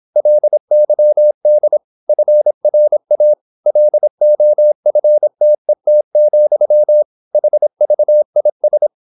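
Morse code, heard as a keyed tone, about 26 wpm, 600 Hz. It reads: LYD FRA LOFTET, HVIS
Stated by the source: High-speed (QRQ) Morse at 26 wpm.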